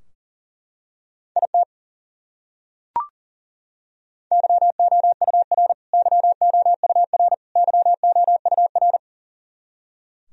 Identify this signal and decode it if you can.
Morse code, heard as a keyed tone, about 40 wpm, 700 Hz. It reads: IT YOUR YOUR YOUR
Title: Top 100 Words Repeated 40wpm